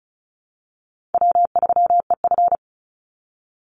Morse code, heard as a keyed tone, about 34 wpm, 700 Hz. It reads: W3EF
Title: Easy Call Signs Repeated 2x - 35wpm